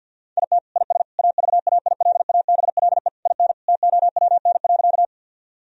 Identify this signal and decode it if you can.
Morse code, heard as a keyed tone, about 50 wpm, 700 Hz. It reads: IT IS AVAILABLE IN TOWN.
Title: Sentences from Top 1000 Words - Review Part 3 - 50wpm